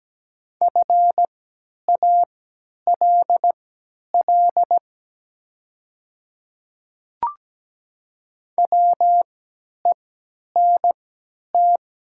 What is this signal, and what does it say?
Morse code, heard as a keyed tone, about 17 wpm, spaced as for 10 wpm, 700 Hz. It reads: FALL WENT
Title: 3x Character Spacing - Top 500 Words - 17wpm